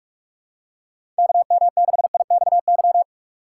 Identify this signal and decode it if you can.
Morse code, heard as a keyed tone, about 42 wpm, 700 Hz. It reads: KM6IXY
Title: USA Call Signs - 2x Character Spacing - 45wpm